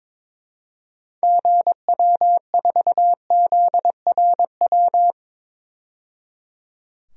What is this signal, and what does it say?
Morse code, heard as a keyed tone, about 22 wpm, 700 Hz. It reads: GW4ZRW
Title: DXCC-100 Call Signs - 6 Characters - 22wpm